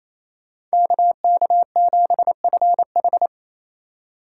Morse code, heard as a keyed tone, about 36 wpm, 700 Hz. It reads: KK7FH